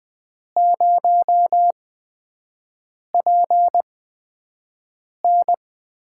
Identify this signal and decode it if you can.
Morse code, heard as a keyed tone, about 20 wpm, spaced as for 5 wpm, 700 Hz. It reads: 0PN
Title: USA Call Signs - 4 Characters - 8x Character Spacing - 20wpm